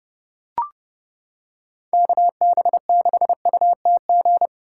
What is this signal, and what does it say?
Morse code, heard as a keyed tone, about 30 wpm, 700 Hz. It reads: KB6UTG